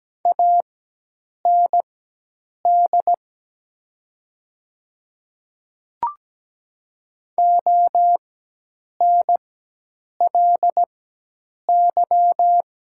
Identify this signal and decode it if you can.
Morse code, heard as a keyed tone, about 17 wpm, spaced as for 8 wpm, 700 Hz. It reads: AND ONLY
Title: 4x Character Spacing - Top 300 Words - 17wpm